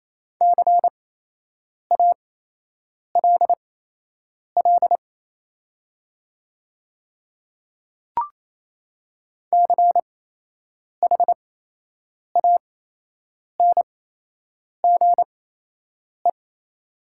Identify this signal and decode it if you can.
Morse code, heard as a keyed tone, about 27 wpm, spaced as for 8 wpm, 700 Hz. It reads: CALL CHANGE
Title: 8x Character Spacing - Top 500 Words - 28wpm